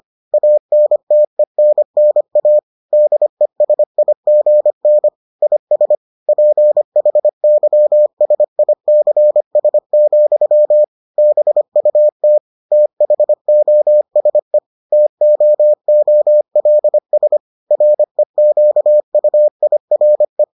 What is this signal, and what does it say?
Morse code, heard as a keyed tone, about 25 wpm, 600 Hz. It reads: ANTENNA DESIGN IS PHYSICS, BUT THOSE TOOLS REQUIRE